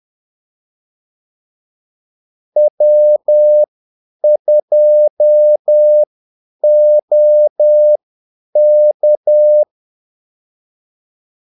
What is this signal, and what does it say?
Morse code, heard as a keyed tone, about 10 wpm, spaced as for 8 wpm, 600 Hz. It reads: W2OK
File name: Callsigns-10-8wpm